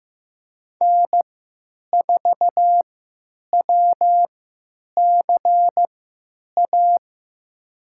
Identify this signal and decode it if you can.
Morse code, heard as a keyed tone, about 15 wpm, spaced as for 9 wpm, 700 Hz. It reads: N4WCA
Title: USA Call Signs - 5 Characters - 3x Character Spacing - 15wpm